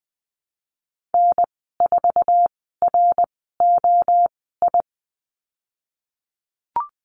Morse code, heard as a keyed tone, about 20 wpm, spaced as for 15 wpm, 700 Hz. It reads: N4ROI
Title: 069 Lesson 12g - Letter d - Call Signs 20wpm